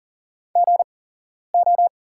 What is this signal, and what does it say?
Morse code, heard as a keyed tone, about 40 wpm, spaced as for 11 wpm, 700 Hz. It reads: GO
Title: 8x Character Spacing - Top 100 Words - 40wpm